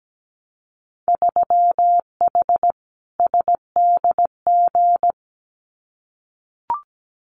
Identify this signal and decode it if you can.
Morse code, heard as a keyed tone, about 17 wpm, 700 Hz. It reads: 3H SDG
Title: Winter Field Day Exchanges Encore 17wpm